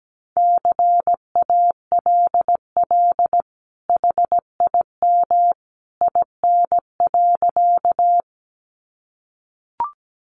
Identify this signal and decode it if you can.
Morse code, heard as a keyed tone, about 17 wpm, 700 Hz. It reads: CALL HIM IN.
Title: Sentences from Top 100 Words 17wpm